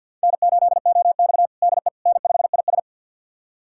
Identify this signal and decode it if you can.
Morse code, heard as a keyed tone, about 49 wpm, 700 Hz. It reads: N8OX DE N5IS
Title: USA Callsign DE Callsign - 4 Characters - 50wpm